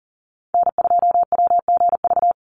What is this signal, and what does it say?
Morse code, heard as a keyed tone, about 40 wpm, 700 Hz. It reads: N2WGV
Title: USA Call Signs 40wpm